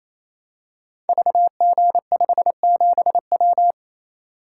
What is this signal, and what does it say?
Morse code, heard as a keyed tone, about 28 wpm, 700 Hz. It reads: VG57W
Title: Code Groups - 5 Letter:Numbers - 28wpm